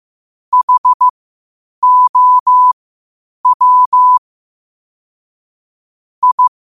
Morse code, heard as a keyed tone, about 15 wpm, spaced as for 8 wpm, 1000 Hz. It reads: HOW I
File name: qotdfiles / 8wpm